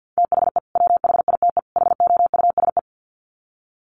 Morse code, heard as a keyed tone, about 50 wpm, 700 Hz. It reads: THE WHITE HOUSE
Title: Sets of 3 Words 50wpm